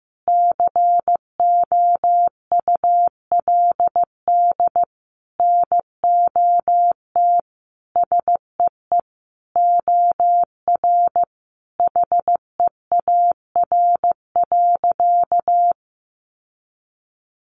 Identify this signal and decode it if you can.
Morse code, heard as a keyed tone, about 15 wpm, 700 Hz. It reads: COULD NOT SEE OR HEAR.
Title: Sentences from Top 600 Words 15wpm